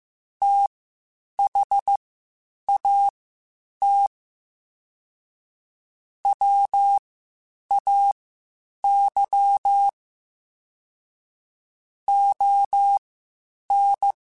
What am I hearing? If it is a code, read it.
Morse code, heard as a keyed tone, about 15 wpm, spaced as for 8 wpm, 800 Hz. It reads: THAT WAY ON
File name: (spaced) / QSOs Spaced-mp3s / WA7QOX 1 @13 sp